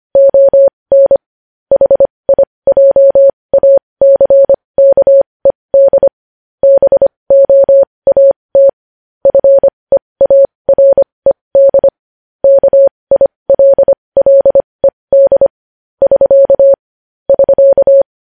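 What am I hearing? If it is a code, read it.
Morse code, heard as a keyed tone, about 25 wpm, spaced as for 20 wpm, 550 Hz. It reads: ON HIJACKED BOAT FEARED KILLED <SK> <SK>
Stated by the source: Morse code news headlines sent at 20 wpm.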